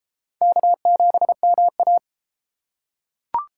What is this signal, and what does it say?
Morse code, heard as a keyed tone, about 33 wpm, 700 Hz. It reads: K7MA